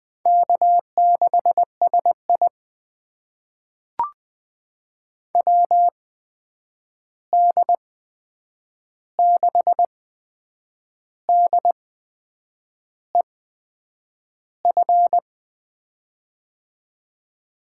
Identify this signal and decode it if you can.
Morse code, heard as a keyed tone, about 20 wpm, spaced as for 8 wpm, 700 Hz. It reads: K6SI WD6DEF
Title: USA Call Signs - 8x Character Spacing - 20wpm